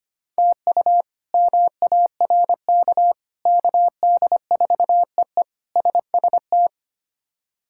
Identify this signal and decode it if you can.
Morse code, heard as a keyed tone, about 25 wpm, 700 Hz. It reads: TU MARK KD4EE SST
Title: K1USN SST QSO Practice - 25wpm